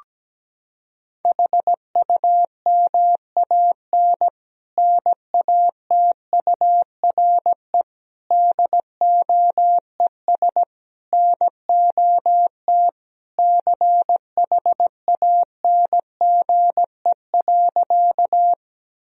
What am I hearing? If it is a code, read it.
Morse code, heard as a keyed tone, about 17 wpm, 700 Hz. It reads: HUMAN NATURE DOES NOT CHANGE.
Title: Sentences from Top 1000 Words - Review Part 3 - 17wpm